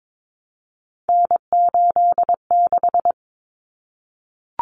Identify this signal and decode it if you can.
Morse code, heard as a keyed tone, about 22 wpm, 700 Hz. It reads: N86
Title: Common Cut Numbers - 3-Digit 22wpm